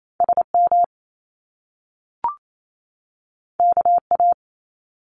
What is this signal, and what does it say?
Morse code, heard as a keyed tone, about 28 wpm, 700 Hz. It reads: SM KA
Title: Top Two Letter Combinations in Words 28wpm